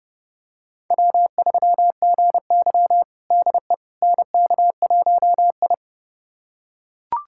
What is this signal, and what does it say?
Morse code, heard as a keyed tone, about 30 wpm, 700 Hz. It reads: W3GY DE NK1I
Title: USA Callsign DE Callsign - 4 Characters - 30wpm